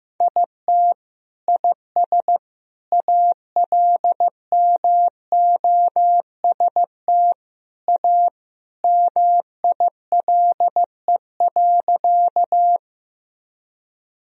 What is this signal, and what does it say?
Morse code, heard as a keyed tone, about 15 wpm, 700 Hz. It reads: IT IS ALMOST A MILE.
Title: Sentences from Top 1000 Words - Review Part 4 - 15wpm